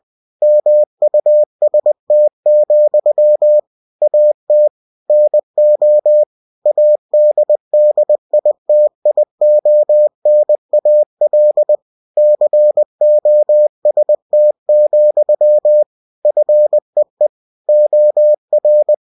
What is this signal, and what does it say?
Morse code, heard as a keyed tone, about 20 wpm, 600 Hz. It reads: MUST, AT NO ADDITIONAL COST, FEE OR
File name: cw-books / winnie-the-pooh / winnie-20wpm-page-0116